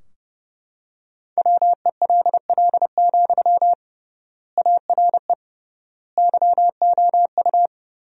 Morse code, heard as a keyed tone, about 30 wpm, 700 Hz. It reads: WELL, ARE YOU
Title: 3x Word-Spacing - Sentences from Top 400 Words - 30wpm